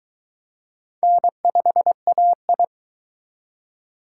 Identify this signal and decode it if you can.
Morse code, heard as a keyed tone, about 22 wpm, 700 Hz. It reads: N5AI